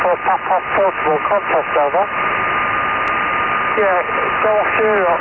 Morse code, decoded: none